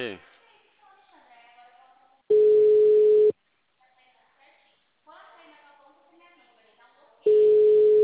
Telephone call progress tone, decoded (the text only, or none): ringback tone